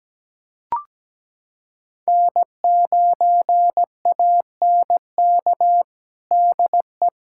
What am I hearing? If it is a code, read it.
Morse code, heard as a keyed tone, about 17 wpm, 700 Hz. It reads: N9ANK DE
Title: USA Callsign DE Callsign 17wpm